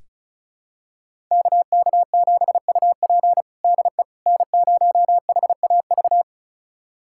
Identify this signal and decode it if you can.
Morse code, heard as a keyed tone, about 36 wpm, 700 Hz. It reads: KK7UP DE N0HAV
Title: USA Callsign DE Callsign - 5 Characters - 35wpm